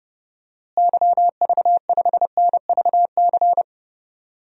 Morse code, heard as a keyed tone, about 32 wpm, 700 Hz. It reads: YV5NVC
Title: DXCC-100 Call Signs - 6 Characters - Repeated 3x - 30wpm